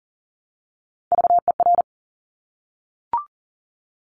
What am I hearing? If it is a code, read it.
Morse code, heard as a keyed tone, about 42 wpm, 700 Hz. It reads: VER
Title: Top Three Letter Combinations in Words 40wpm